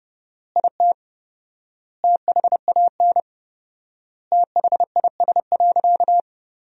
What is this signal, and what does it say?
Morse code, heard as a keyed tone, about 30 wpm, 700 Hz. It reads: IT THAN THIS.
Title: 4x Word-Spacing - Sentences from Top 100 Words - 30wpm